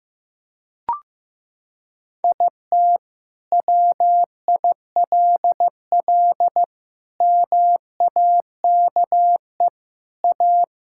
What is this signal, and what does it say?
Morse code, heard as a keyed tone, about 16 wpm, 700 Hz. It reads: IT WILL MAKE A
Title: Sentences from Top 1000 Words - Review Part 5 - 15wpm